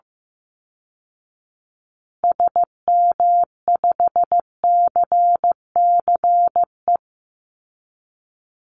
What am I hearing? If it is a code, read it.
Morse code, heard as a keyed tone, about 15 wpm, 700 Hz. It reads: SM5CCE
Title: Call Signs Repeated 3x - 15wpm